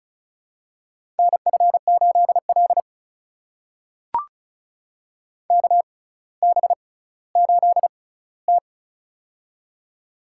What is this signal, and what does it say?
Morse code, heard as a keyed tone, about 36 wpm, spaced as for 18 wpm, 700 Hz. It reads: NF8L KB8T
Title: USA Call Signs - 4 Characters - 6x Character Spacing - 35wpm